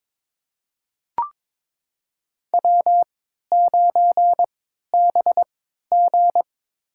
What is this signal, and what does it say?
Morse code, heard as a keyed tone, about 22 wpm, spaced as for 13 wpm, 700 Hz. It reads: W9BG
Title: USA Call Signs - 4 Characters - 3x Character Spacing - 22wpm